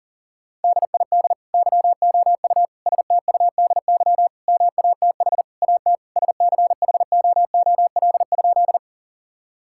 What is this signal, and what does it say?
Morse code, heard as a keyed tone, about 40 wpm, 700 Hz. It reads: DID YOU STUDY MATH AT SCHOOL?
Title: Sentences from Top 1000 Words - Review Part 6 - 40wpm